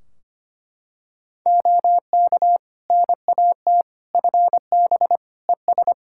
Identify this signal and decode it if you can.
Morse code, heard as a keyed tone, about 25 wpm, 700 Hz. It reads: OK NAT FB ES
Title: Practice QSO - Exchange 2 - Info - 25wpm